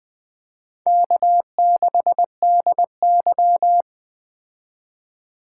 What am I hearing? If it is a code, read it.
Morse code, heard as a keyed tone, about 20 wpm, 700 Hz. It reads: K6DY
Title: Speed Racing - USA 1x2 Call Signs 17wpm